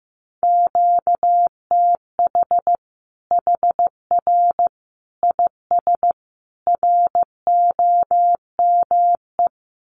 Morse code, heard as a keyed tone, about 15 wpm, 700 Hz. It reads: QTH HR IS ROME
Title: Common QSO Phrases 15wpm